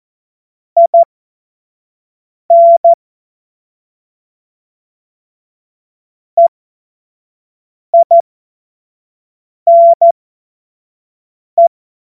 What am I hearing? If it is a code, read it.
Morse code, heard as a keyed tone, about 14 wpm, spaced as for 5 wpm, 680 Hz. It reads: IN EINE
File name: Morsekurs / Witze / Witz 18. Mai 2021, 25 F70